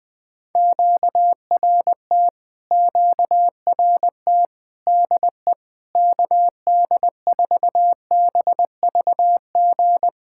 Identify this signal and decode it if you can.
Morse code, heard as a keyed tone, about 21 wpm, 700 Hz. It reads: QRT QRT DE KD4BVG
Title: POTA QSO Practice - 20wpm